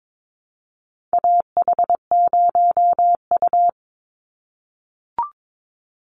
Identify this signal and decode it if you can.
Morse code, heard as a keyed tone, about 22 wpm, 700 Hz. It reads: AH0U
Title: Call Signs - 22wpm